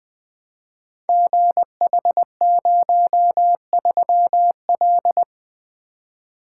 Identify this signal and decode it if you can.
Morse code, heard as a keyed tone, about 21 wpm, 700 Hz. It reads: GH03L
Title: Code Groups - 5 Letter:Numbers - 20wpm